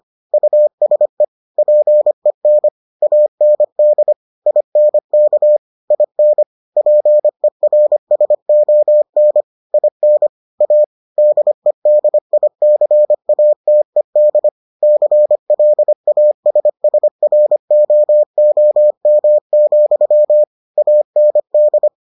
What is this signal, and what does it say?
Morse code, heard as a keyed tone, about 25 wpm, 600 Hz. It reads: USE PEN AND INK IN PERSON IN A DEDICATED CLASSROOM, AND